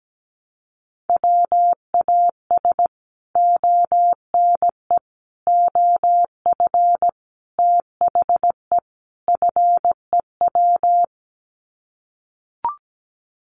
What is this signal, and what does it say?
Morse code, heard as a keyed tone, about 17 wpm, 700 Hz. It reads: WAS ONE OF THE FEW
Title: Sets of 5 Words Encore 17wpm